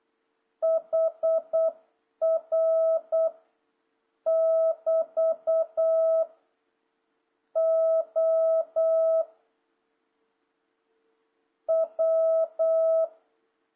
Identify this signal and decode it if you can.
Morse code, heard as a keyed tone, about 8 wpm, 650 Hz. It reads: HR <BT> O W
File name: jh8cbh / 2026cw